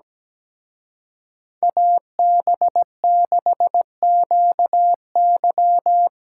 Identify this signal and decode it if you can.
Morse code, heard as a keyed tone, about 19 wpm, 700 Hz. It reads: AB6QY
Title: Speed Racing - USA Call Signs 17wpm